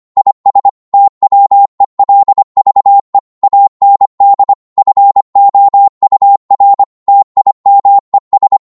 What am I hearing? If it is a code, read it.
Morse code, heard as a keyed tone, about 25 wpm, 800 Hz. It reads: IS TWELVE AND FOUR TIMES